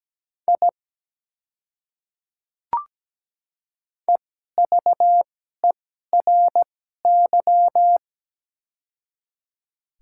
Morse code, heard as a keyed tone, about 17 wpm, spaced as for 12 wpm, 700 Hz. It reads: I EVERY
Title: 2x Character Spacing - Top 200 Words - 17wpm